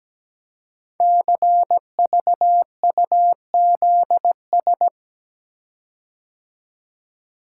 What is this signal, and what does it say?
Morse code, heard as a keyed tone, about 17 wpm, 700 Hz. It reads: CVUZS
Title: Code Groups - 5 Letters - 17wpm